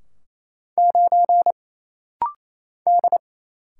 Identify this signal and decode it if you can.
Morse code, heard as a keyed tone, about 28 wpm, 700 Hz. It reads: 9 D
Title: Single Letter-Number - Rapid-Fire 28wpm